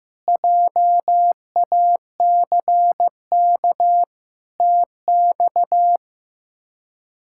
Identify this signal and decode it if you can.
Morse code, heard as a keyed tone, about 16 wpm, 700 Hz. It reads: JACK TX
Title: K1USN SST QSO Practice - 15wpm